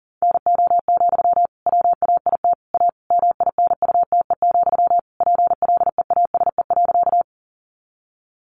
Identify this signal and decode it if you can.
Morse code, heard as a keyed tone, about 40 wpm, 700 Hz. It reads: NO, WAIT A MINUTE, PLEASE.
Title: Sentences from Top 1500 Words 40wpm